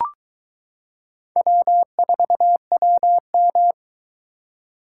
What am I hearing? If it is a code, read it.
Morse code, heard as a keyed tone, about 22 wpm, 700 Hz. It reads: W4WM